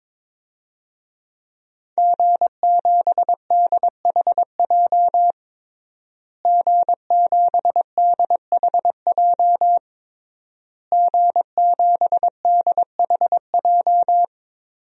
Morse code, heard as a keyed tone, about 22 wpm, 700 Hz. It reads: G7DHJ G7DHJ G7DHJ
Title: DXCC-100 Call Signs - 5 Characters - Repeated 3x - 22wpm